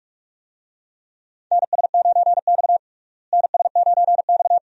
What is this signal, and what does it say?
Morse code, heard as a keyed tone, about 46 wpm, 700 Hz. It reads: NS9X NS9X
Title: USA Call Signs - 4 Characters - Repeated 2x - 45wpm